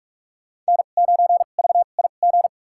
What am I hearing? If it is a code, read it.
Morse code, heard as a keyed tone, about 43 wpm, spaced as for 32 wpm, 700 Hz. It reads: N9VIG
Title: USA Call Signs - 5 Characters - 2x Character Spacing - 45wpm